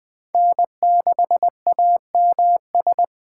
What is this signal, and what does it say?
Morse code, heard as a keyed tone, about 21 wpm, 700 Hz. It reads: N6AMS
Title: Speed Racing - USA 1x3 Call Signs 17wpm